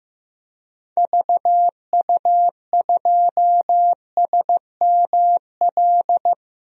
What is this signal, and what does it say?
Morse code, heard as a keyed tone, about 16 wpm, 700 Hz. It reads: VU2SML
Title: DXCC-100 Call Signs - 6 Characters - 15wpm